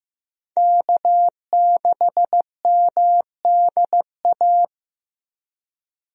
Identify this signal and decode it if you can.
Morse code, heard as a keyed tone, about 15 wpm, 700 Hz. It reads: K6MDA
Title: USA Call Signs - 5 Characters - Repeated 3x - 15wpm